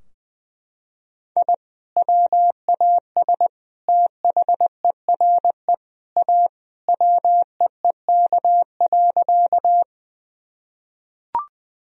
Morse code, heard as a keyed tone, about 21 wpm, 700 Hz. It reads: I WAS THERE A WEEK.
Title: Sentences from Top 1000 Words - Review Part 2 - 20wpm